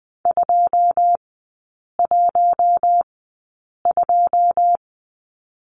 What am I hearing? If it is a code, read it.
Morse code, heard as a keyed tone, about 20 wpm, spaced as for 8 wpm, 700 Hz. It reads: 212
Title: 127 Lesson 21e - Number 6 - Three Characters 20wpm